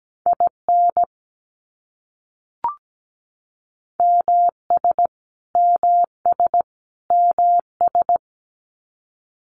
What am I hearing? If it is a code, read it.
Morse code, heard as a keyed tone, about 17 wpm, 700 Hz. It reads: IN MS MS MS
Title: US States Repeated 3x 17wpm